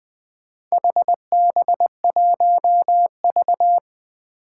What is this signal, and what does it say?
Morse code, heard as a keyed tone, about 20 wpm, 700 Hz. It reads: HB1V
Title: Four Challenging Letter-Number Combinations 20wpm